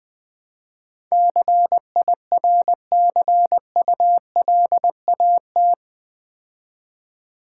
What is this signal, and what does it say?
Morse code, heard as a keyed tone, about 20 wpm, 700 Hz. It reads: CIRCULAT
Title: Build A Word - 4 Letters At A Time - 11-Letter Words - 20wpm